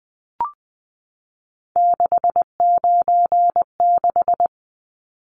Morse code, heard as a keyed tone, about 20 wpm, 700 Hz. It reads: 696
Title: Numbers - 3-Digit 20wpm